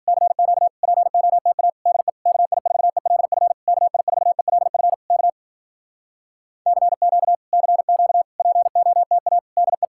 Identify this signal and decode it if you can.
Morse code, heard as a keyed tone, about 52 wpm, 700 Hz. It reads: CQ POTA DE KI4ELF KI4ELF K CQ CQ POTA DE